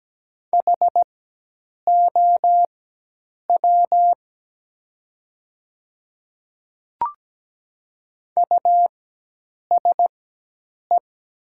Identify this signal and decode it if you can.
Morse code, heard as a keyed tone, about 17 wpm, spaced as for 8 wpm, 700 Hz. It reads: HOW USE
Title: 4x Character Spacing - Top 100 Words - 17wpm